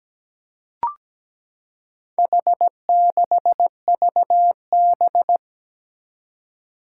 Morse code, heard as a keyed tone, about 17 wpm, 700 Hz. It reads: H6VB
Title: Four Challenging Letter-Number Combinations 17wpm